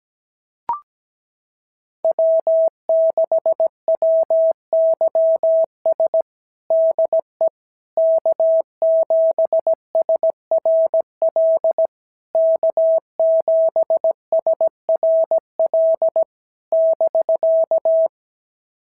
Morse code, heard as a keyed tone, about 17 wpm, 650 Hz. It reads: W6WYS DE K7SRL K7SRL <BK>
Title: SOTA QSO Practice 17wpm